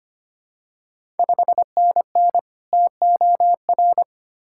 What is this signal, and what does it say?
Morse code, heard as a keyed tone, about 25 wpm, 700 Hz. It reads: 5NN TOR
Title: Ontario QSO Party Exchange 25wpm